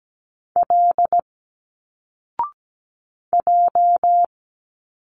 Morse code, heard as a keyed tone, about 17 wpm, 700 Hz. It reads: L J
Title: Single Letters 17wpm